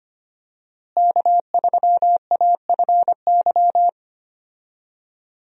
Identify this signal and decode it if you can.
Morse code, heard as a keyed tone, about 25 wpm, 700 Hz. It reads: K3AFY
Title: USA Call Signs - 5 Characters - Repeated 2x - 25wpm